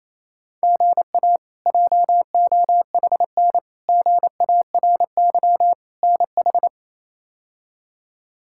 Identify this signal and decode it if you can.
Morse code, heard as a keyed tone, about 28 wpm, 700 Hz. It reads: GA JOHN GARY NH
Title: K1USN SST QSO Practice - 28wpm